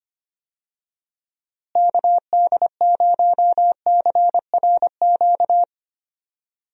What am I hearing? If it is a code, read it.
Morse code, heard as a keyed tone, about 25 wpm, 700 Hz. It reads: KD0CRQ
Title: USA Call Signs - 6 Characters - Repeated 3x - 25wpm